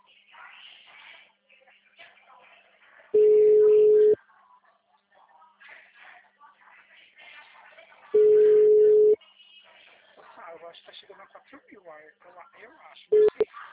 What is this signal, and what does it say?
Ringback tone (Europe, 425 Hz), with speech.